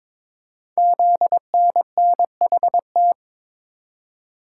Morse code, heard as a keyed tone, about 23 wpm, 700 Hz. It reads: ZNNHT
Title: Code Groups - 5 Letters - 22wpm